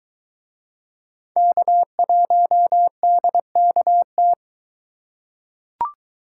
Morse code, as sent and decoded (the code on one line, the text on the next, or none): -.- .---- -.. -.- -
K1DKT